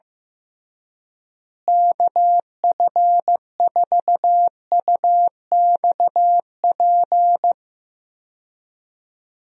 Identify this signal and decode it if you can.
Morse code, heard as a keyed tone, about 15 wpm, 700 Hz. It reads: KF4UXP
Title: USA Call Signs - 6 Characters - Repeated 3x - 15wpm